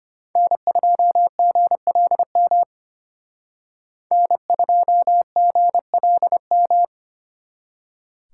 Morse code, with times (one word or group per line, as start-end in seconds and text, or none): 0.4-2.6 N2GLM
4.1-6.8 N2GLM